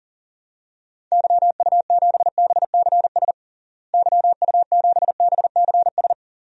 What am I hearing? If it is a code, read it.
Morse code, heard as a keyed tone, about 40 wpm, 700 Hz. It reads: YU7BCS YU7BCS
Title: DXCC-100 Call Signs - 6 Characters - Repeated 2x - 40wpm